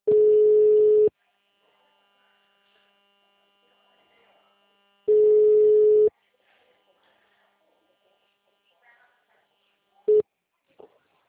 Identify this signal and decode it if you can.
Ringback tone (Europe, 425 Hz).
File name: out- -2001